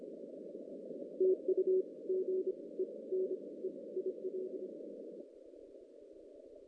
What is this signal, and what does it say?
Morse code, heard as a keyed tone, about 26 wpm, spaced as for 21 wpm, 370 Hz. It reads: TU GENE IL
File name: 2016 JAN CW NAQP / 2016 Jan naqp k5ot - 0-9